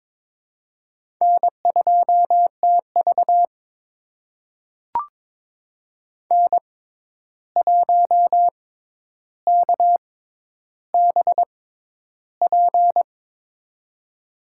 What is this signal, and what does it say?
Morse code, heard as a keyed tone, about 22 wpm, spaced as for 11 wpm, 700 Hz. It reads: N2TV N1KBP
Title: USA Call Signs - 6x Character Spacing - 22wpm